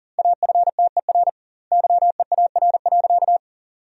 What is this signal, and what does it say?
Morse code, heard as a keyed tone, about 37 wpm, 700 Hz. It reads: AFTER YEAR.
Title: 2x Word-Spacing - Sentences from Top 300 Words - 40wpm